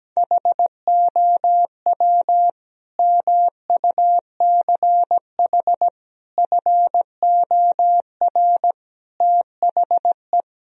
Morse code, heard as a keyed tone, about 17 wpm, 700 Hz. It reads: HOW MUCH FOR THE
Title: Sentences from Top 1000 Words - Review Part 3 - 17wpm